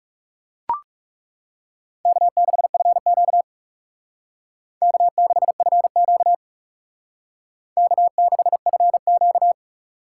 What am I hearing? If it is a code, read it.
Morse code, heard as a keyed tone, about 38 wpm, 700 Hz. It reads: K6FQ K6FQ K6FQ